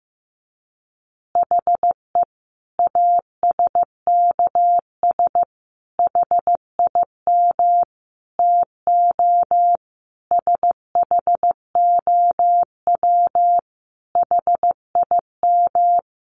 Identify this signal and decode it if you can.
Morse code, heard as a keyed tone, about 15 wpm, 700 Hz. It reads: HE ASKS HIM TO SHOW HIM
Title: Easy Phrases 15wpm